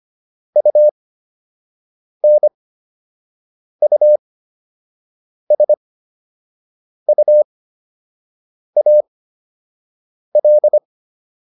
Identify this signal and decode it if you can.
Morse code, heard as a keyed tone, about 25 wpm, spaced as for 6 wpm, 600 Hz. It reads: UNUSUAL